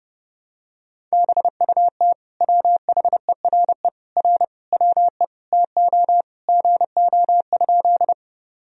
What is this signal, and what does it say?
Morse code, heard as a keyed tone, about 30 wpm, 700 Hz. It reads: BUT WHERE R WE TO GO?
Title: Sentences from Top 200 Words and Abbreviations 30wpm